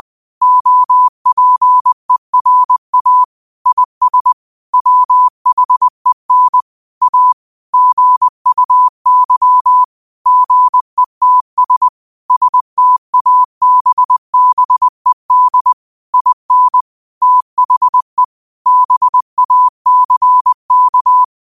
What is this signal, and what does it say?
Morse code, heard as a keyed tone, about 20 wpm, 1000 Hz. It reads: OPERA IS WHEN A GUY GETS STABBED IN THE BACK
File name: qotdfiles / 20wpm